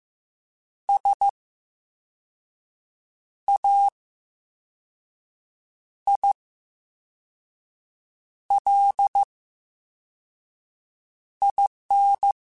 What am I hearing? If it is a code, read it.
Morse code, heard as a keyed tone, about 15 wpm, spaced as for 4 wpm, 800 Hz. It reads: SAILIN
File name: Special INGs / Special INGs-mp3s / Slow ing @13